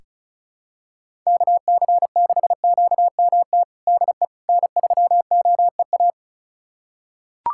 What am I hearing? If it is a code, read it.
Morse code, heard as a keyed tone, about 36 wpm, 700 Hz. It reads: KC6QMT DE N3OEA
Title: USA Callsign DE Callsign 35wpm